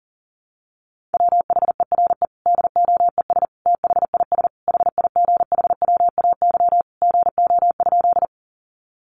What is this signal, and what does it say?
Morse code, heard as a keyed tone, about 40 wpm, 700 Hz. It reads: WHERE DOES THIS HIGHWAY GO?
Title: Sentences from Top 1400 Words 40wpm